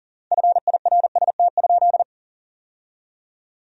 Morse code, heard as a keyed tone, about 40 wpm, 700 Hz. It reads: FIRST?
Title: 3x Word-Spacing - Sentences from Top 100 Words - 40wpm